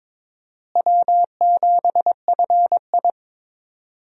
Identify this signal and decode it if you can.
Morse code, heard as a keyed tone, about 22 wpm, 700 Hz. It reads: W7FI
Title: USA Call Signs - 4 Characters - Repeated 3x - 22wpm